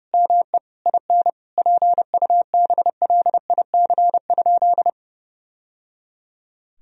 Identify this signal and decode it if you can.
Morse code, heard as a keyed tone, about 30 wpm, 700 Hz. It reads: ME IN PUBLIC?
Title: Sentences from Top 1000 Words - Review Part 3 - 30wpm